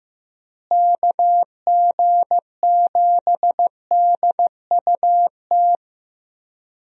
Morse code, heard as a keyed tone, about 15 wpm, 700 Hz. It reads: KG7DUT
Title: USA Call Signs - 6 Characters - Repeated 3x - 15wpm